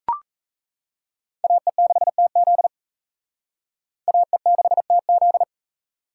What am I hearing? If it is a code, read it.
Morse code, heard as a keyed tone, about 34 wpm, 700 Hz. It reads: AE6TZ AE6TZ
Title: Speed Racing - USA 2x2 Call Signs 28wpm